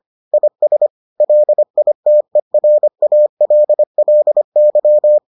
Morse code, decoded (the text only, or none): IS LITERALLY